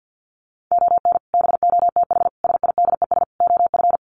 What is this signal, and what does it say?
Morse code, heard as a keyed tone, about 50 wpm, 700 Hz. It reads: ON BOTH SIDES OF
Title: Sets of 4 Words 50wpm